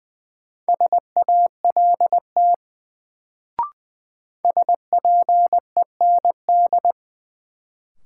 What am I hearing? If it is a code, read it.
Morse code, heard as a keyed tone, about 21 wpm, 700 Hz. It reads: SALT SPEND
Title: Sets of 5 Words - New Word Drill - 20wpm